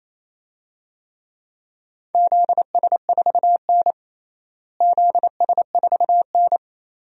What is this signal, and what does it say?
Morse code, heard as a keyed tone, about 28 wpm, 700 Hz. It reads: ZS4N ZS4N
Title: DXCC-100 Call Signs - 4 Characters - Repeated 3x - 28wpm